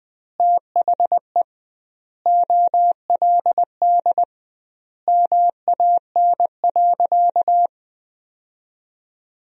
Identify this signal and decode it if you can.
Morse code, heard as a keyed tone, about 20 wpm, 700 Hz. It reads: THE OLD MAN.
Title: 2x Word-Spacing - Sentences from Top 500 Words - 20wpm